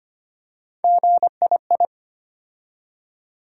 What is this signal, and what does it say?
Morse code, heard as a keyed tone, about 25 wpm, 700 Hz. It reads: GII
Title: USA Call Sign Postfixes - 3 Letters 25wpm